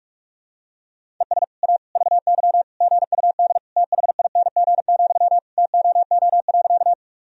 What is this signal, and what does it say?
Morse code, heard as a keyed tone, about 41 wpm, 700 Hz. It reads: ES A VY GUD THING, TOO.
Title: Sentences from Top 200 Words and Abbreviations 45wpm